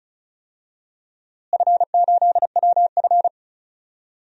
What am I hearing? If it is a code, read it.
Morse code, heard as a keyed tone, about 36 wpm, 700 Hz. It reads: F8WF
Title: DXCC-100 Call Signs - 4 Characters - Repeated 2x - 35wpm